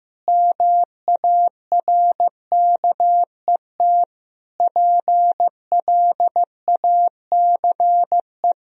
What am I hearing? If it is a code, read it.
Morse code, heard as a keyed tone, about 15 wpm, 700 Hz. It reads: MARKET PLACE
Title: Sentences from Top 1000 Words - Review Part 3 - 15wpm